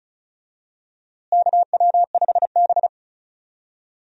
Morse code, heard as a keyed tone, about 34 wpm, 700 Hz. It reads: KW5B
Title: USA Call Signs - 4 Characters - Repeated 3x - 35wpm